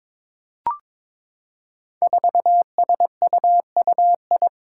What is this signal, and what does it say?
Morse code, heard as a keyed tone, about 21 wpm, 700 Hz. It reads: 4SUUI
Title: Five Challenging Letter-Number Combinations 22wpm